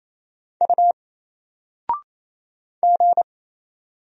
Morse code, heard as a keyed tone, about 27 wpm, 700 Hz. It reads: U G
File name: Single-Letter-Number-28wpm